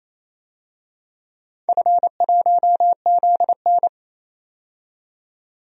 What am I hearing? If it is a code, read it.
Morse code, heard as a keyed tone, about 28 wpm, 700 Hz. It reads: F1ZN
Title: DXCC-100 Call Signs - 4 Characters - 28wpm